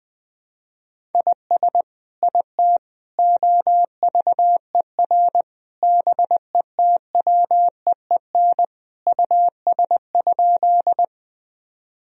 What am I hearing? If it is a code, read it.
Morse code, heard as a keyed tone, about 21 wpm, 700 Hz. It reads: IS IT OVER BETWEEN US?
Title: Sentences from Top 1000 Words - Review Part 1 - 20wpm